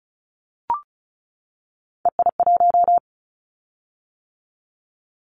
Morse code, heard as a keyed tone, about 36 wpm, 700 Hz. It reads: EI1
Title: Callsign Prefixes for Top 50 DXCC - 35wpm